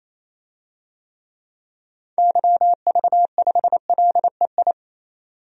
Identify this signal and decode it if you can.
Morse code, heard as a keyed tone, about 28 wpm, 700 Hz. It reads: YV5LEI
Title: DXCC-100 Call Signs - 6 Characters - 28wpm